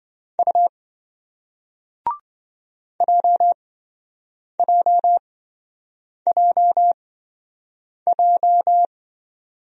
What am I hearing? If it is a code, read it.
Morse code, heard as a keyed tone, about 28 wpm, spaced as for 7 wpm, 700 Hz. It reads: U JJJJ